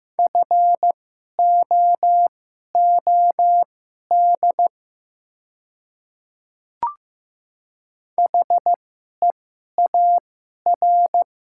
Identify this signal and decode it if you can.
Morse code, heard as a keyed tone, about 15 wpm, spaced as for 11 wpm, 700 Hz. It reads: FOOD HEAR
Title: 2x Character Spacing - Top 500 Words - 15wpm